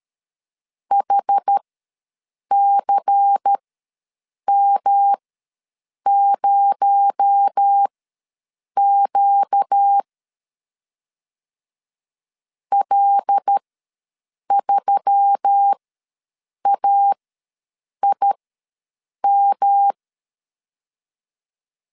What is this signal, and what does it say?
Morse code, heard as a keyed tone, about 13 wpm, spaced as for 7 wpm, 800 Hz. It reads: HCM0Q L3AIM
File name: LEZIONI / 6 PKX0 / 6 PKX0 ws 12wpm 3448